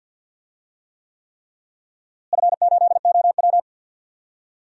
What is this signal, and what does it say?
Morse code, heard as a keyed tone, about 51 wpm, 700 Hz. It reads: F8OW
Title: DXCC-100 Call Signs - 4 Characters - Repeated 3x - 50wpm